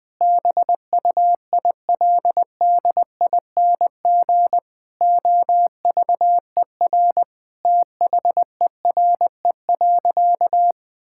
Morse code, heard as a keyed tone, about 21 wpm, 700 Hz. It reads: BUILDING OVER THERE.
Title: Sentences from Top 1000 Words - Review Part 5 - 20wpm